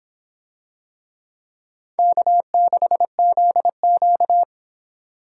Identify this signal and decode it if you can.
Morse code, heard as a keyed tone, about 25 wpm, 700 Hz. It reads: K6ZQ